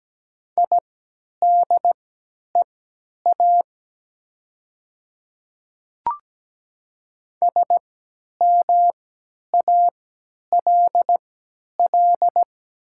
Morse code, heard as a keyed tone, about 17 wpm, spaced as for 10 wpm, 700 Hz. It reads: IDEA SMALL
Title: 3x Character Spacing - Top 500 Words - 17wpm